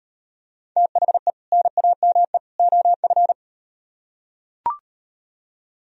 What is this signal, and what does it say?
Morse code, heard as a keyed tone, about 35 wpm, 700 Hz. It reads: THE NAME OF